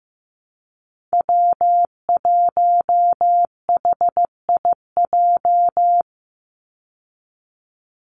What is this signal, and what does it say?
Morse code, heard as a keyed tone, about 15 wpm, 700 Hz. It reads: W1HIJ
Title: Call Signs Repeated 2x - 15wpm